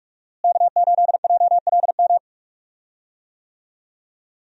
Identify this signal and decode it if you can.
Morse code, heard as a keyed tone, about 45 wpm, 700 Hz. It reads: K8JLM
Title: USA Call Signs - 5 Characters - Repeated 2x - 45wpm